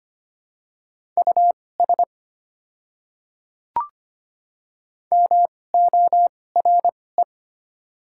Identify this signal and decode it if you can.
Morse code, heard as a keyed tone, about 25 wpm, spaced as for 18 wpm, 700 Hz. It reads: US MORE
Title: 2x Character Spacing - Top 200 Words - 25wpm